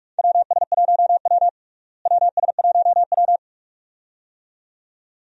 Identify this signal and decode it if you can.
Morse code, heard as a keyed tone, about 42 wpm, 700 Hz. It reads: WS1W WS1W
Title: USA Call Signs - 4 Characters - Repeated 2x - 45wpm